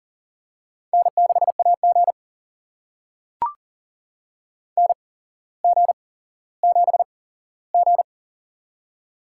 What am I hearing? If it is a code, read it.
Morse code, heard as a keyed tone, about 41 wpm, spaced as for 17 wpm, 700 Hz. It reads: N6AG NG7G
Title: USA Call Signs - 4 Characters - 8x Character Spacing - 40wpm